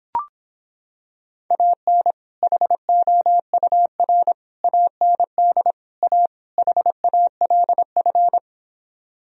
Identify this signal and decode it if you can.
Morse code, heard as a keyed tone, about 26 wpm, 700 Hz. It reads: AN HOUR AND A HALF